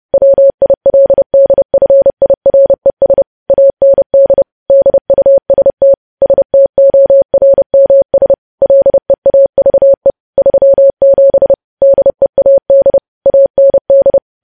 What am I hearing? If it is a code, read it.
Morse code, heard as a keyed tone, about 30 wpm, 550 Hz. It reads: WILDFIRES AND DUST STORMS LEAVE 37 DEAD AND
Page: News Headlines in Morse Code at 30 WPM